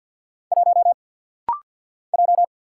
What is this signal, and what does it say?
Morse code, heard as a keyed tone, about 49 wpm, 700 Hz. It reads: 1 J
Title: Single Letter-Number - Rapid-Fire 50wpm